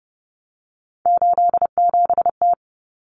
Morse code, heard as a keyed tone, about 30 wpm, 700 Hz. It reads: 87T
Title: Common Cut Numbers - 3-Digit 30wpm